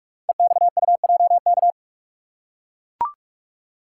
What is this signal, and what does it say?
Morse code, heard as a keyed tone, about 44 wpm, 700 Hz. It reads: EXUJK
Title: Code Groups - 5 Letters - 45wpm